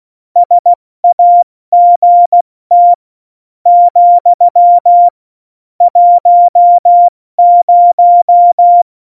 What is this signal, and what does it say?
Morse code, heard as a keyed tone, about 16 wpm, 700 Hz. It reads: SAGT , 10
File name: Morsekurs / Witze / Witz 11. September 2018, 70 F80